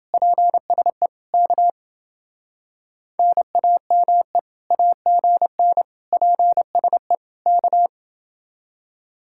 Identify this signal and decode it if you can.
Morse code, heard as a keyed tone, about 28 wpm, 700 Hz. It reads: PSE K NAME AGN PSE K